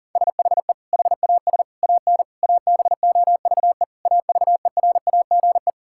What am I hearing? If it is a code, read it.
Morse code, heard as a keyed tone, about 40 wpm, 700 Hz. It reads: SHE HAS AN ABOVE AVERAGE
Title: Sentences from Top 1000 Words - Review Part 6 - 40wpm